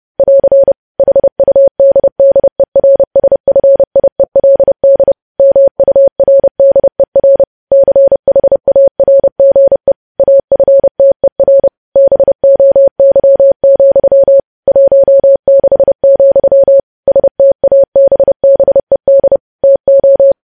Morse code, heard as a keyed tone, about 30 wpm, 550 Hz. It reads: <AR> HUDDERSFIELD MURDER CHARGE AFTER BOY, 16, STABBED TO
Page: news headlines in morse code at 30 wpm